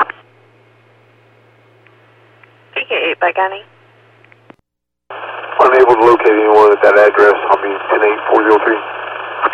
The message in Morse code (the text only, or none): none